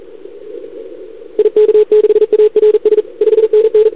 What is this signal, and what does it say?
Morse code, heard as a keyed tone, about 37 wpm, 440 Hz. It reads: IK6ARS 5NN
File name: CQWWCW2021 / Doderna OL3Z